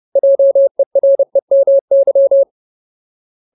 Morse code, heard as a keyed tone, about 30 wpm, 550 Hz. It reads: JEREMY
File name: Ultimate Word List / 30WPM Ultimate